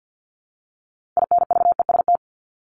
Morse code, heard as a keyed tone, about 50 wpm, 700 Hz. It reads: INVEST
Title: Top 1001-1250 Words 50wpm